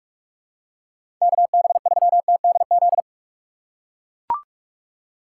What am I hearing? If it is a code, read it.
Morse code, heard as a keyed tone, about 43 wpm, 700 Hz. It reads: KB3TDZ
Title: USA Call Signs - 6 Characters - 4x Character Spacing - 45wpm